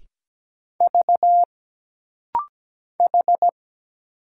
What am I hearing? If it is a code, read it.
Morse code, heard as a keyed tone, about 17 wpm, 700 Hz. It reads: V H
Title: Single Letter-Number - Rapid-Fire 17wpm